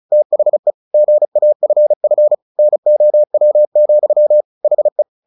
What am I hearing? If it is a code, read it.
Morse code, heard as a keyed tone, about 36 wpm, 600 Hz. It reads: THE GAFF NOW, HE
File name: the old man and the sea 2 35wpm